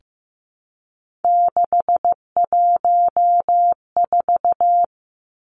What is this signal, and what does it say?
Morse code, heard as a keyed tone, about 15 wpm, 700 Hz. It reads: 614
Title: Numbers - 3-Digit 15wpm